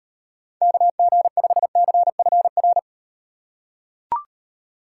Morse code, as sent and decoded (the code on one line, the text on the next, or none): -.- --. ..... -.-. ..-. .-.
KG5CFR